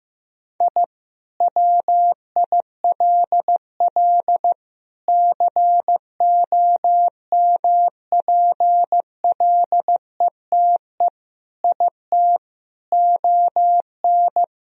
Morse code, heard as a keyed tone, about 16 wpm, 700 Hz. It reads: I WILL COMPLETE IT ON
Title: Sentences from Top 1000 Words - Review Part 2 - 15wpm